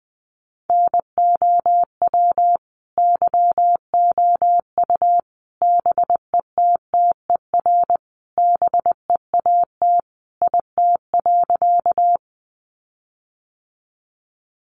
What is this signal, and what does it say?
Morse code, heard as a keyed tone, about 20 wpm, 700 Hz. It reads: NOW YOU BETTER BEAT IT.
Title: Sentences from Top 1300 Words 20wpm